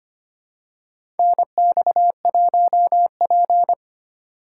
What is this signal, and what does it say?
Morse code, heard as a keyed tone, about 25 wpm, 700 Hz. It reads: NX1P
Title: USA Call Signs - 4 Characters - 3x Character Spacing - 25wpm